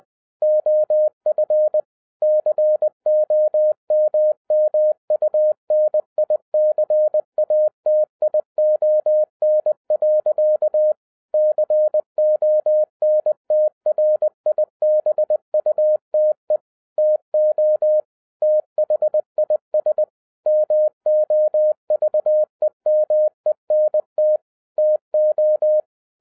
Morse code, heard as a keyed tone, about 20 wpm, 610 Hz. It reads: OF COMMUNICATION. CONTRIBUTE TO THIS MOVEMENT TO